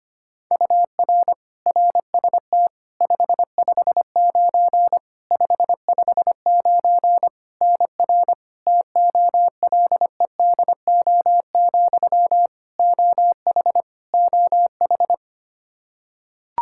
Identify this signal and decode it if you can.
Morse code, heard as a keyed tone, about 25 wpm, 700 Hz. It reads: UR RST 559 559 NR TOLEDO, OH OH
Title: Practice QSO - Exchange 1 - Beginning - 25wpm